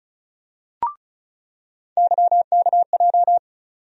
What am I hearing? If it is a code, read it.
Morse code, heard as a keyed tone, about 34 wpm, 700 Hz. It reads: YKJ
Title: USA Call Sign Postfixes - 3 Letters 35wpm